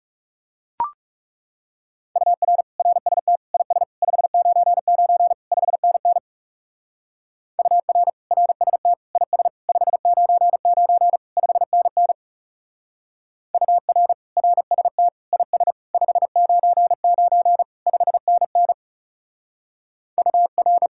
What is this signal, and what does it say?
Morse code, heard as a keyed tone, about 37 wpm, 700 Hz. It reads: UR RST IS 599 5NN UR RST IS 599 5NN UR RST IS 599 5NN UR
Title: Speed-Racing Common QSO Phrases Encore - 30wpm